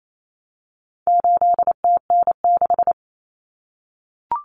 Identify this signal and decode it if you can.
Morse code, heard as a keyed tone, about 28 wpm, 700 Hz. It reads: 8TN6
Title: Common Cut Numbers - 4-Digit 28wpm